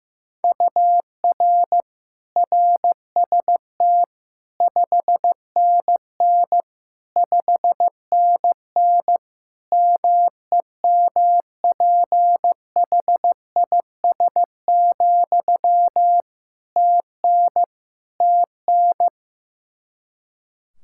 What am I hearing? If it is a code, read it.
Morse code, heard as a keyed tone, about 16 wpm, 700 Hz. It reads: UR RST 5NN 5NN MEMPHIS, TN TN
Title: Practice QSO - Exchange 1 - Beginning - 15wpm